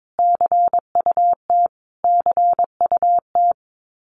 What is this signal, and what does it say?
Morse code, heard as a keyed tone, about 22 wpm, 700 Hz. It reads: CUT CUT
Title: Top 300 Words Repeated 22wpm